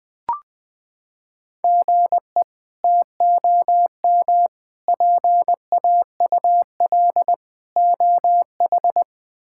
Morse code, heard as a keyed tone, about 21 wpm, 700 Hz. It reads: GE TOM PAUL OH
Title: K1USN SST QSO Practice - 20wpm